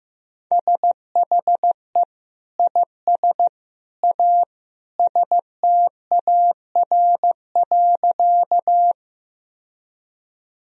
Morse code, heard as a keyed tone, about 15 wpm, 700 Hz. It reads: SHE IS A STAR.
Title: Sentences from Top 1000 Words - Review Part 2 - 15wpm